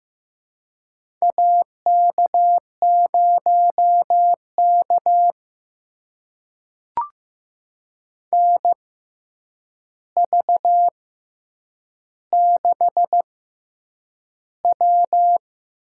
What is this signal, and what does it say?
Morse code, heard as a keyed tone, about 15 wpm, spaced as for 8 wpm, 700 Hz. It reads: AK0K NV6W
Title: USA Call Signs - 4 Characters - 6x Character Spacing - 15wpm